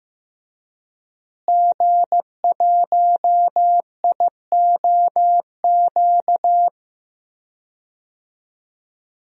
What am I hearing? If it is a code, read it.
Morse code, heard as a keyed tone, about 15 wpm, 700 Hz. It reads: G1IOQ
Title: DXCC-100 Call Signs - 5 Characters - Repeated 3x - 15wpm